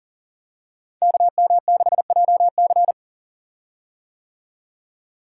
Morse code, heard as a keyed tone, about 39 wpm, 690 Hz. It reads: KM6JC
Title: Code Groups - 5 Letter:Numbers - 40wpm